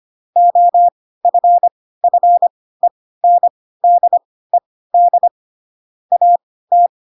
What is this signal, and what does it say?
Morse code, heard as a keyed tone, about 25 wpm, spaced as for 16 wpm, 700 Hz. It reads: OFFENDED AT
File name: Pinocchio / Pinocchio ch4 10 25 ik1hge